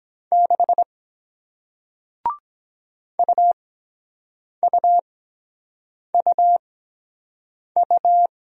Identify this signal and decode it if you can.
Morse code, heard as a keyed tone, about 23 wpm, spaced as for 7 wpm, 700 Hz. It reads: 6 UUUU